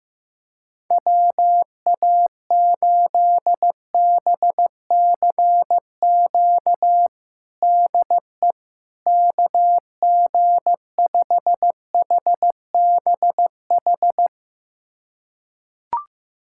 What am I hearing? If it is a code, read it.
Morse code, heard as a keyed tone, about 15 wpm, 700 Hz. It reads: WA8BCQ DE KG5HBH
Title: USA Callsign DE Callsign - 6 Characters - 15wpm